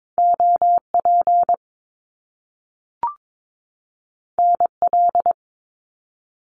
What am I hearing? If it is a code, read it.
Morse code, heard as a keyed tone, about 22 wpm, 700 Hz. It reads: OP NL
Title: Top Two Letter Combinations in Words 22wpm